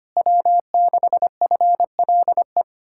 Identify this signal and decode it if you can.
Morse code, heard as a keyed tone, about 32 wpm, 700 Hz. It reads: W6FLE